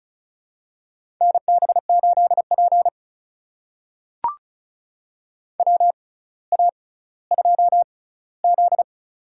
Morse code, heard as a keyed tone, about 36 wpm, spaced as for 18 wpm, 700 Hz. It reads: NB8P WA2Z
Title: USA Call Signs - 4 Characters - 6x Character Spacing - 35wpm